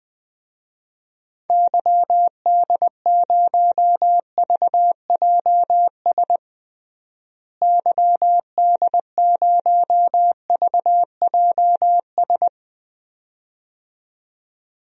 Morse code, heard as a keyed tone, about 20 wpm, 700 Hz. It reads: YD0VJS YD0VJS
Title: DXCC-100 Call Signs - 6 Characters - Repeated 2x - 20wpm